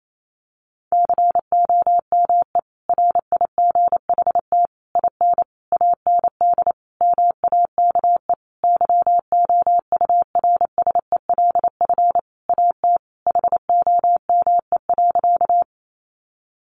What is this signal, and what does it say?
Morse code, heard as a keyed tone, about 28 wpm, 700 Hz. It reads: COME RIGHT IN AND MAKE YOURSELF AT HOME.
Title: Sentences from Top 600 Words 28wpm